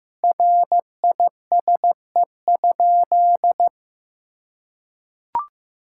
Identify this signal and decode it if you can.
Morse code, heard as a keyed tone, about 16 wpm, 700 Hz. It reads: RISE?
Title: Sentences from Top 1000 Words - Review Part 6 - 15wpm